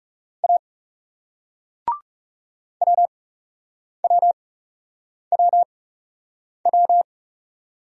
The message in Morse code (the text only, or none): A WWWW